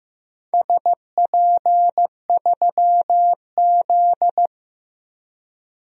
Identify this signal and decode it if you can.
Morse code, heard as a keyed tone, about 15 wpm, 700 Hz. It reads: SP3Z
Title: DXCC-100 Call Signs - 4 Characters - 15wpm